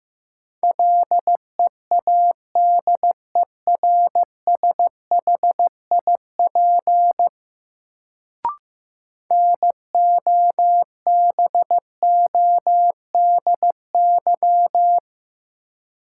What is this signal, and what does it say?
Morse code, heard as a keyed tone, about 15 wpm, 700 Hz. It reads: LEADERSHIP NOBODY
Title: Sentences from Top 1600 Words - New Word Drill - 15wpm